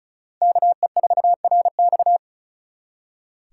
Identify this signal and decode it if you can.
Morse code, heard as a keyed tone, about 43 wpm, 700 Hz. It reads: KE4RX
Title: Speed Racing - USA 2x2 Call Signs 35wpm